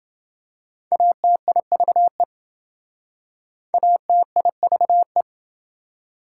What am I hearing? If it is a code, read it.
Morse code, heard as a keyed tone, about 27 wpm, 700 Hz. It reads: ATIVE ATIVE